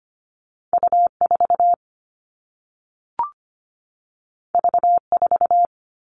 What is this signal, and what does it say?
Morse code, heard as a keyed tone, about 25 wpm, 700 Hz. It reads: U4 V4
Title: All Cut Numbers - 2-Digit 25wpm